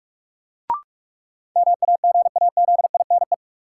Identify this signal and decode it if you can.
Morse code, heard as a keyed tone, about 46 wpm, 700 Hz. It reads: MAGAZINE
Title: Sentences from Top 1000 Words - Review Part 7 - New Word Drill - 45wpm